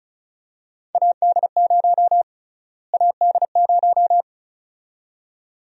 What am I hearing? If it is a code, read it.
Morse code, heard as a keyed tone, about 36 wpm, 700 Hz. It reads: AD0 AD0
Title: USA Call Sign Prefixes - Repeated 2x - 35wpm